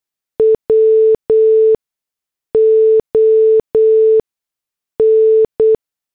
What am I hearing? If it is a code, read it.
Morse code, heard as a keyed tone, about 8 wpm, spaced as for 6 wpm, 440 Hz. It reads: WON
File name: morse / headlines